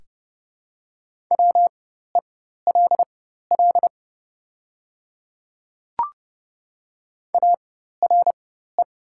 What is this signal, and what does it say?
Morse code, heard as a keyed tone, about 30 wpm, spaced as for 14 wpm, 700 Hz. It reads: WELL ARE
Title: 4x Character Spacing - Top 300 Words - 30wpm